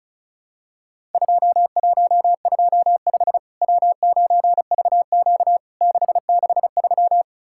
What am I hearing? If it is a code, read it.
Morse code, heard as a keyed tone, about 36 wpm, 700 Hz. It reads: 2125 W9VQ 663
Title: Worked All Europe DX Contest - QTC Exchange - 35wpm